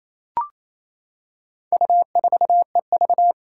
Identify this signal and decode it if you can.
Morse code, heard as a keyed tone, about 27 wpm, 700 Hz. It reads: U4EV
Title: Four Challenging Letter-Number Combinations 28wpm